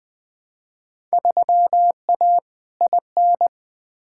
Morse code, heard as a keyed tone, about 20 wpm, 700 Hz. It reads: 3A IN
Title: Field Day Exchanges Encore - 20wpm